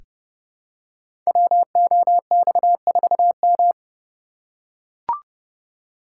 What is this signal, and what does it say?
Morse code, heard as a keyed tone, about 30 wpm, 700 Hz. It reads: WOX4M
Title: Code Groups - 5 Letter:Numbers - 30wpm